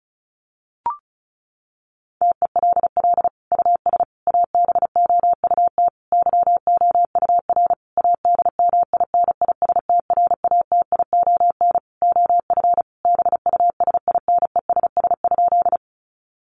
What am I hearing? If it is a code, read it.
Morse code, heard as a keyed tone, about 34 wpm, 700 Hz. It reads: TELL US ABOUT YOUR ADMINISTRATION OF BUSINESS?
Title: Sentences from Top 1300 Words 35wpm